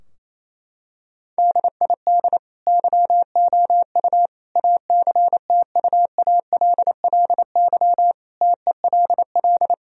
Morse code, heard as a keyed tone, about 28 wpm, 700 Hz. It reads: DID YOU ACTUALLY TELL
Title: Sentences from Top 1000 Words - Review Part 4 - 28wpm